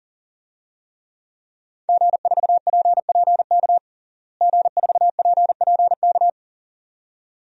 Morse code, heard as a keyed tone, about 40 wpm, 700 Hz. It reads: G4PPK G4PPK
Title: DXCC-100 Call Signs - 5 Characters - Repeated 2x - 40wpm